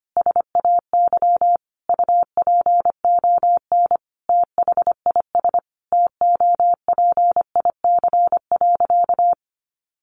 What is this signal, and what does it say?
Morse code, heard as a keyed tone, about 25 wpm, 700 Hz. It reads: SAY UPON THIS TOPIC.
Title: Sentences from Top 1200 Words 25wpm